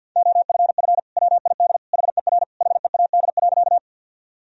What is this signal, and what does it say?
Morse code, heard as a keyed tone, about 46 wpm, 700 Hz. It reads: OFF WID HER HEAD.
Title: Sentences from Top 300 Words and Abbreviations 50wpm